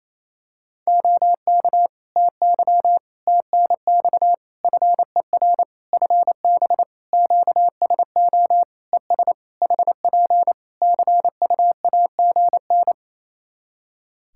Morse code, heard as a keyed tone, about 28 wpm, 700 Hz. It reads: OK TY TNX FER FB QSO ES HP CUAGN
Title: K1USN SST QSO Practice 28wpm